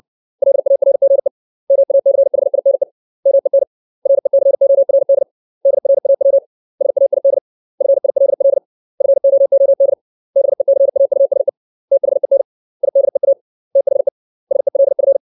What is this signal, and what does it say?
Morse code, heard as a keyed tone, about 60 wpm, 550 Hz. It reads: LARGE MACHINE MEN RECORD DRAW SAID FILL FOOD BECAUSE THEN IDEA THE SELF
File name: 632 words / 632 Words 60WPM